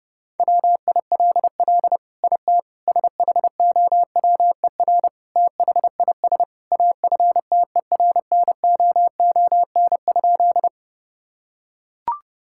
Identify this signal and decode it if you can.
Morse code, heard as a keyed tone, about 30 wpm, 700 Hz. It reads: WILL IT SHOWER THIS AFTERNOON?
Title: Sentences from Top 1600 Words 30wpm